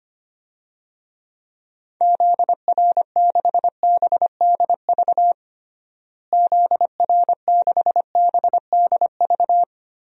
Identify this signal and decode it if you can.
Morse code, heard as a keyed tone, about 25 wpm, 700 Hz. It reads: ZR6BDV ZR6BDV
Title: DXCC-100 Call Signs - 6 Characters - Repeated 2x - 25wpm